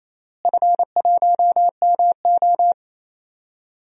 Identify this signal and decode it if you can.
Morse code, heard as a keyed tone, about 28 wpm, 700 Hz. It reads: F1MO
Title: DXCC-100 Call Signs - 4 Characters - Repeated 3x - 28wpm